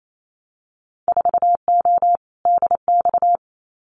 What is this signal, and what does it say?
Morse code, heard as a keyed tone, about 28 wpm, 700 Hz. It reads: 4O DX
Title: Winter Field Day Exchanges Encore 28wpm